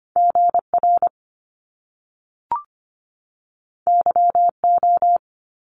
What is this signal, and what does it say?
Morse code, heard as a keyed tone, about 25 wpm, 700 Hz. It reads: GR YO
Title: Top Two Letter Combinations in Words 25wpm